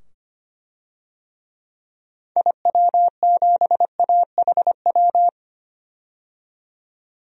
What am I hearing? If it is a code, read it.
Morse code, heard as a keyed tone, about 25 wpm, 700 Hz. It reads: IW7AHW
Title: DXCC-100 Call Signs - 6 Characters - 25wpm